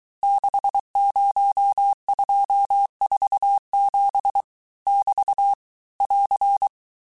Morse code, heard as a keyed tone, about 23 wpm, 800 Hz. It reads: 60247 <BT> <AR>
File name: cours-du-mercredi / audios / cours-855-texte-chiffre-vitesse-22-mpm